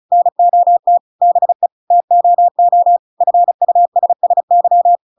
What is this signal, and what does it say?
Morse code, heard as a keyed tone, about 34 wpm, 700 Hz. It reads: NOT BE TOO FUSSY
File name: Pinocchio / Pinocchio ch7 35 35 ik1hge